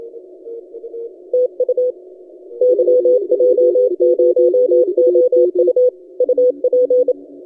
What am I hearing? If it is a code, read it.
Morse code, heard as a keyed tone, about 27 wpm, 500 Hz. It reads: TU YJ0OU UP
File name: YJ0OU 12CW